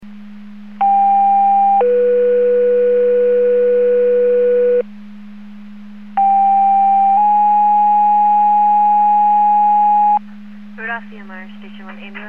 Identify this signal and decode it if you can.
Two-tone page, heard as a one-second tone, then a three-second tone, about 800 then 490 Hz, about 790 then 830 Hz, with speech. It is heard from 0.8 to 10.2 s.